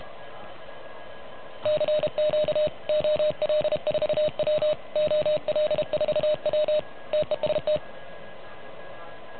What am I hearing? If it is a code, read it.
Morse code, heard as a keyed tone, about 33 wpm, 610 Hz. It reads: CQ OL4W OL4W TEST
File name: truSDX / MWC29.8